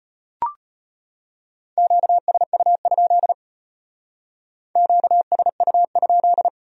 Morse code, heard as a keyed tone, about 33 wpm, 700 Hz. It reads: QSU? QSU?